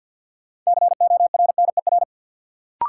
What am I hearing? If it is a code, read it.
Morse code, heard as a keyed tone, about 49 wpm, 700 Hz. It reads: CORNER
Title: Sentences from Top 1600 Words - New Word Drill - 50wpm